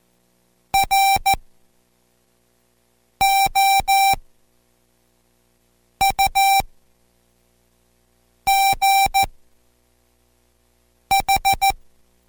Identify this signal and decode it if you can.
Morse code, heard as a keyed tone, about 14 wpm, spaced as for 4 wpm, 780 Hz. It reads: ROUGH